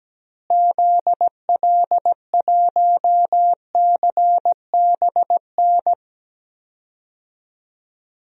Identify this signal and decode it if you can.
Morse code, heard as a keyed tone, about 17 wpm, 700 Hz. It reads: ZL1CBN
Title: DXCC-100 Call Signs - 6 Characters - Repeated 3x - 17wpm